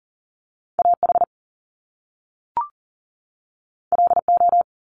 Morse code, heard as a keyed tone, about 40 wpm, 700 Hz. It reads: AH LO
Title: Top Two Letter Combinations in Words 40wpm